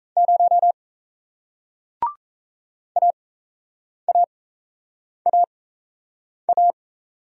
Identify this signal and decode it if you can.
Morse code, heard as a keyed tone, about 38 wpm, spaced as for 8 wpm, 700 Hz. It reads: 0 AAAA